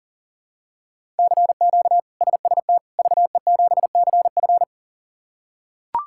Morse code, heard as a keyed tone, about 40 wpm, 700 Hz. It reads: CQ SST VE7CF
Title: K1USN SST QSO Practice - 40wpm